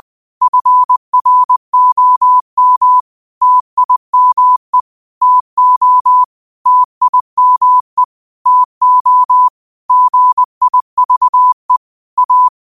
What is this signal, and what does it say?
Morse code, heard as a keyed tone, about 20 wpm, 1000 Hz. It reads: FROM TIME TO TIME TO GIVE A
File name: qotdfiles / 20wpm